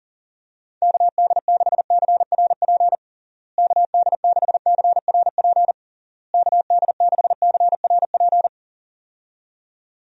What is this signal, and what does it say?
Morse code, heard as a keyed tone, about 40 wpm, 700 Hz. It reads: KD6CRP KD6CRP KD6CRP
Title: USA Call Signs - 6 Characters - Repeated 3x - 40wpm